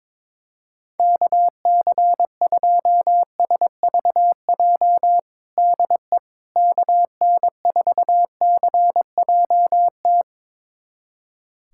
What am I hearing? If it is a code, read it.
Morse code, heard as a keyed tone, about 22 wpm, 700 Hz. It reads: KC2SVJ DE KN4CJT
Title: USA Callsign DE Callsign - 6 Characters - 22wpm